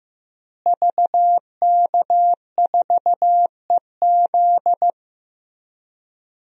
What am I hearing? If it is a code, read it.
Morse code, heard as a keyed tone, about 15 wpm, 700 Hz. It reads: VK4EZ
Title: DXCC-100 Call Signs - 5 Characters - Repeated 3x - 15wpm